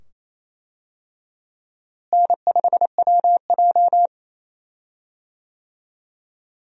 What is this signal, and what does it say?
Morse code, heard as a keyed tone, about 28 wpm, 700 Hz. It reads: N5WJ
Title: USA Call Signs - 4 Characters - Repeated 2x - 28wpm